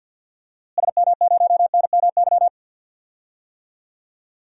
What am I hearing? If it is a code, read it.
Morse code, heard as a keyed tone, about 45 wpm, 700 Hz. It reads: SM0NMY
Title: DXCC-100 Call Signs - 6 Characters - Repeated 2x - 50wpm